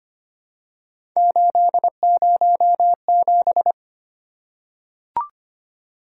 Morse code, heard as a keyed tone, about 25 wpm, 700 Hz. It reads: 807
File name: Three Letter-Number Combinations - 25wpm